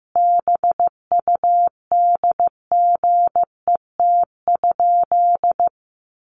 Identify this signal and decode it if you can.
Morse code, heard as a keyed tone, about 15 wpm, 700 Hz. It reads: BUDGET?
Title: Sentences from Top 1200 Words 15wpm